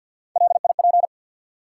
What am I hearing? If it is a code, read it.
Morse code, heard as a keyed tone, about 53 wpm, 700 Hz. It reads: LIP
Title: Three Letter Words - 50wpm